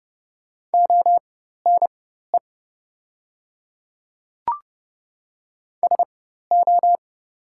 Morse code, heard as a keyed tone, about 30 wpm, spaced as for 14 wpm, 700 Hz. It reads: ONE SO
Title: 4x Character Spacing - Top 100 Words - 30wpm